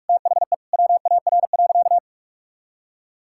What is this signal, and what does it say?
Morse code, heard as a keyed tone, about 41 wpm, 700 Hz. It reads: THE WAR.
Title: Sentences from Top 1000 Words - Review Part 6 - 45wpm